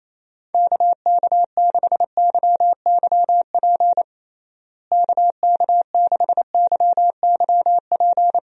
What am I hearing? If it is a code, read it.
Morse code, heard as a keyed tone, about 28 wpm, 700 Hz. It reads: KK6YYP KK6YYP
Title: USA Call Signs - 6 Characters - Repeated 2x - 28wpm